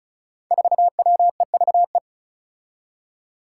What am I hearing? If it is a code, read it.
Morse code, heard as a keyed tone, about 36 wpm, 700 Hz. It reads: 4WEVE
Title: Five Challenging Letter-Number Combinations 35wpm